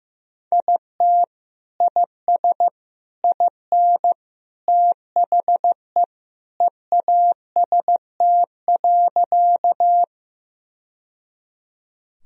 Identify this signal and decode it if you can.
Morse code, heard as a keyed tone, about 16 wpm, 700 Hz. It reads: IT IS IN THE EAST.
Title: Sentences from Top 1000 Words - Review Part 3 - 15wpm